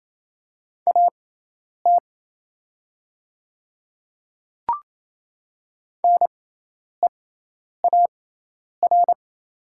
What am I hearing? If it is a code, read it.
Morse code, heard as a keyed tone, about 27 wpm, spaced as for 10 wpm, 700 Hz. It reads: AT NEAR
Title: 6x Character Spacing - Top 200 Words - 28wpm